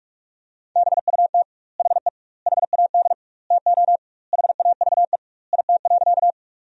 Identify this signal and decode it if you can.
Morse code, heard as a keyed tone, about 46 wpm, 700 Hz. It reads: BUT HE HAD TO HAVE IT.
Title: 2x Word-Spacing - Sentences from Top 100 Words - 45wpm